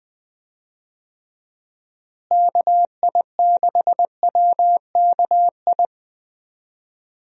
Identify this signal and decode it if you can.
Morse code, heard as a keyed tone, about 20 wpm, 700 Hz. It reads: KI6WKI
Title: USA Call Signs - 6 Characters - Repeated 3x - 20wpm